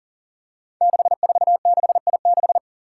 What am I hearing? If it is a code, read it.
Morse code, heard as a keyed tone, about 40 wpm, 700 Hz. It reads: 646I6
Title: Five Challenging Letter-Number Combinations 40wpm